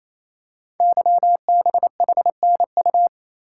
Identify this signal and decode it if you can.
Morse code, heard as a keyed tone, about 29 wpm, 700 Hz. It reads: YBHNU
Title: Code Groups - 5 Letter:Numbers - 28wpm